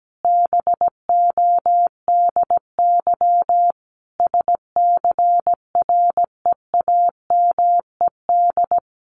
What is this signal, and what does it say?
Morse code, heard as a keyed tone, about 17 wpm, 700 Hz. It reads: BODY SCREAMED
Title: A Sound Of Thunder 17wpm